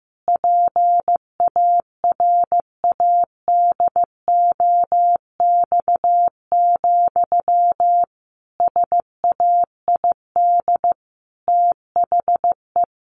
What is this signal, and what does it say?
Morse code, heard as a keyed tone, about 15 wpm, 700 Hz. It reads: PARADOX, SAID THE
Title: A Sound Of Thunder 15wpm